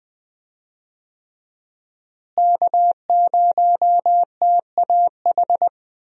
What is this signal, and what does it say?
Morse code, heard as a keyed tone, about 21 wpm, 700 Hz. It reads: K0TAH
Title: USA Call Signs - 5 Characters - Repeated 3x - 20wpm